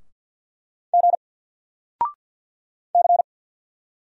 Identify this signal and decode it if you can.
Morse code, heard as a keyed tone, about 48 wpm, 700 Hz. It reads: G C
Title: Single Letter-Number 50wpm